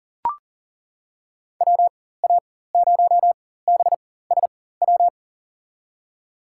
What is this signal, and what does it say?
Morse code, heard as a keyed tone, about 37 wpm, spaced as for 18 wpm, 700 Hz. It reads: WA0BSW
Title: USA Call Signs - 4x Character Spacing - 40wpm